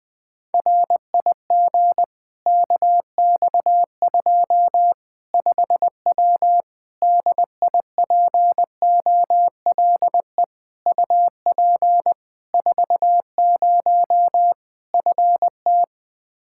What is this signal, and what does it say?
Morse code, heard as a keyed tone, about 20 wpm, 700 Hz. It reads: RIG KX2 5W DIPOLE UP 40 FT
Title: Practice QSO - Exchange 2 - Info - 20wpm